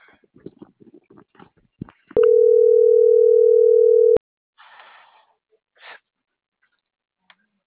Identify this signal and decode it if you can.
Ringback tone (North America), 440 + 480 Hz.